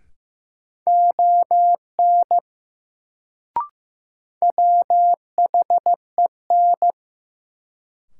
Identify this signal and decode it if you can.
Morse code, heard as a keyed tone, about 15 wpm, 700 Hz. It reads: ON WHEN
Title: Sentences from Top 100 Words - New Word Drill - 15wpm